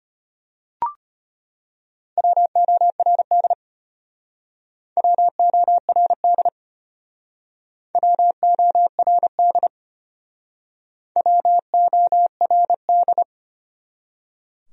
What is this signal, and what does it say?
Morse code, heard as a keyed tone, about 33 wpm, 700 Hz. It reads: WORD WORD WORD WORD